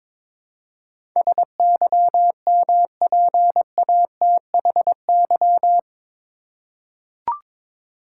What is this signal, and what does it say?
Morse code, heard as a keyed tone, about 22 wpm, 700 Hz. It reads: SYMPATHY
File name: practice / 22wpm / Eight Letter Words - 22wpm